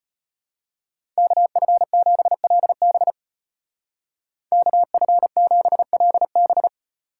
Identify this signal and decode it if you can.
Morse code, heard as a keyed tone, about 30 wpm, 700 Hz. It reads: KF7LB KF7LB
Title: Speed Racing - USA 2x2 Call Signs 25wpm